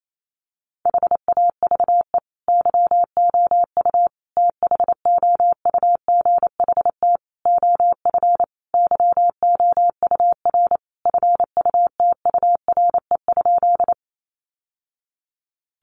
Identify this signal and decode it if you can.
Morse code, heard as a keyed tone, about 28 wpm, 700 Hz. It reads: HAVE YOU THOUGHT OF YOUR FUTURE?
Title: Sentences from Top 700 Words 28wpm